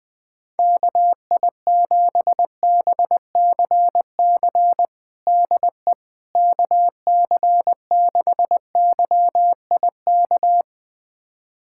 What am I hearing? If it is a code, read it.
Morse code, heard as a keyed tone, about 21 wpm, 700 Hz. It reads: KI7BCC DE KC6YIK
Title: USA Callsign DE Callsign 20wpm